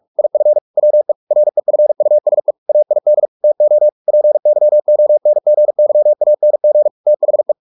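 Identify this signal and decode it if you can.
Morse code, heard as a keyed tone, about 44 wpm, 600 Hz. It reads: IF WE REFUSE AID TO PYONGYANG THE
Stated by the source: Morse practice at 45 wpm, the novel Never by Ken Follett in English, from CW med Gnister.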